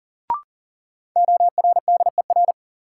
Morse code, heard as a keyed tone, about 40 wpm, 700 Hz. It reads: ORDER
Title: Sets of 2 Words - New Word Drill - 40wpm